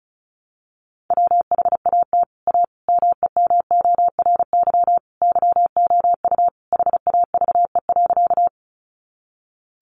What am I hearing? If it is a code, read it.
Morse code, heard as a keyed tone, about 34 wpm, 700 Hz. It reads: WHAT A MEMORY YOU HAVE.
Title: Sentences from Top 1300 Words 35wpm